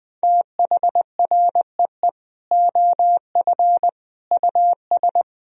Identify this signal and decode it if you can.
Morse code, heard as a keyed tone, about 20 wpm, 700 Hz. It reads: THREE OF US
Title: Sentences from Top 1000 Words - Review Part 1 - 20wpm